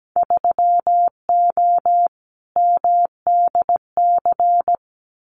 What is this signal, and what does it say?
Morse code, heard as a keyed tone, about 17 wpm, 700 Hz. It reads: 3O MDC
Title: Winter Field Day Exchanges 17wpm